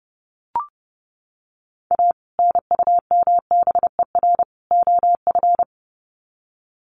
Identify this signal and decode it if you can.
Morse code, heard as a keyed tone, about 30 wpm, 700 Hz. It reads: A NUMBER OF
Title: Sets of 3 Words 30wpm